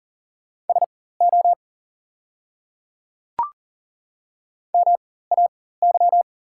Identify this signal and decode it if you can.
Morse code, heard as a keyed tone, about 40 wpm, spaced as for 19 wpm, 690 Hz. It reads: SO MAY
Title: 4x Character Spacing - Top 100 Words - 40wpm